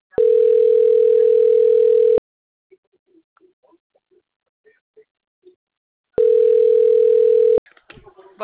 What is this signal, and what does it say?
Ringback tone (North America), 440 + 480 Hz, with speech.
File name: out- - - - - .51254